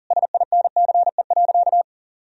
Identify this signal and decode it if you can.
Morse code, heard as a keyed tone, about 40 wpm, 700 Hz. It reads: SINCE.
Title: Sentences from Top 300 Words and Abbreviations 40wpm